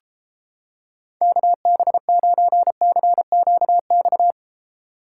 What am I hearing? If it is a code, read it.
Morse code, heard as a keyed tone, about 29 wpm, 700 Hz. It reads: KB9CQX